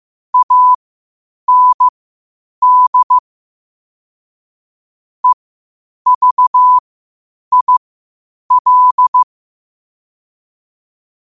Morse code, heard as a keyed tone, about 15 wpm, spaced as for 8 wpm, 1000 Hz. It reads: AND EVIL
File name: qotdfiles / 8wpm